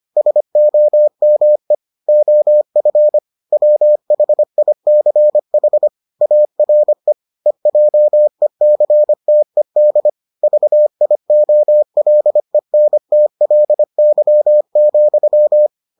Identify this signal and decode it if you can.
Morse code, heard as a keyed tone, about 25 wpm, 600 Hz. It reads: SOME OF WHICH ARE EJECTED VIOLENTLY,